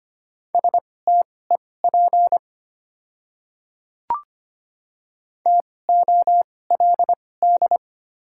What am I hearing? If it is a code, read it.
Morse code, heard as a keyed tone, about 25 wpm, spaced as for 18 wpm, 700 Hz. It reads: STEP TOLD
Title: 2x Character Spacing - Top 500 Words - 25wpm